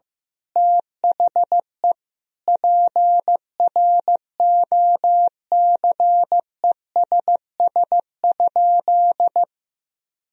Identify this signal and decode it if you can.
Morse code, heard as a keyed tone, about 14 wpm, 700 Hz. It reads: THE PROCESS?
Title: Sentences from Top 1000 Words - Review Part 3 - 15wpm